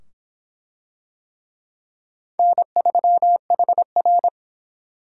Morse code, heard as a keyed tone, about 25 wpm, 700 Hz. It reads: N3HR